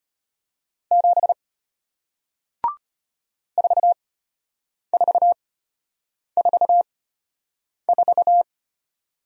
Morse code, heard as a keyed tone, about 34 wpm, spaced as for 8 wpm, 700 Hz. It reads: 7 4444